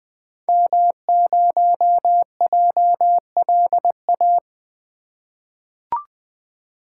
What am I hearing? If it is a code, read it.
Morse code, heard as a keyed tone, about 20 wpm, 700 Hz. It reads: M0JLA
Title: 2023 SOTA Activator Call Signs - 20wpm